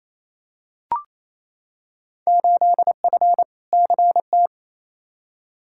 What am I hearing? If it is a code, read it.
Morse code, heard as a keyed tone, about 28 wpm, 700 Hz. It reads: 8F CT
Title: Field Day Exchanges Encore - 28wpm